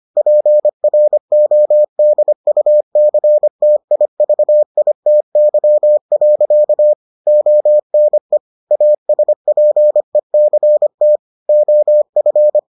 Morse code, heard as a keyed tone, about 25 wpm, 600 Hz. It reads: PRODUCTIVITY. ONE ASPECT OF